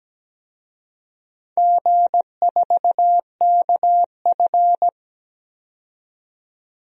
Morse code, heard as a keyed tone, about 17 wpm, 700 Hz. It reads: G4KF
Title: DXCC-100 Call Signs - 4 Characters - 17wpm